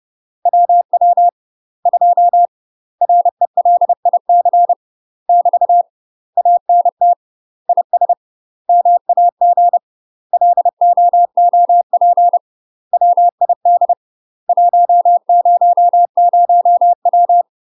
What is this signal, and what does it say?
Morse code, heard as a keyed tone, about 30 wpm, 700 Hz. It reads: WW 2 RELIC <BT> ANT IS MAG LOOP WID 100W
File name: UK MASTER QSOs TRAINING / 30WPM / UK MASTER QSOs 0035